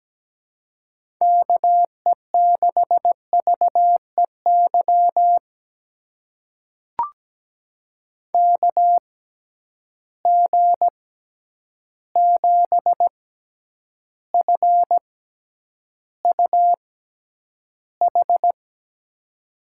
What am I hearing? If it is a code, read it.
Morse code, heard as a keyed tone, about 17 wpm, spaced as for 9 wpm, 700 Hz. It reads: KE6VEY KG7FUH
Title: USA Call Signs - 6 Characters - 6x Character Spacing - 17wpm